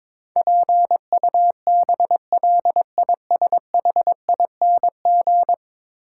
Morse code, heard as a keyed tone, about 22 wpm, 700 Hz. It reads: PUBLISHING
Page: morse-code-ninja-practice-22wpm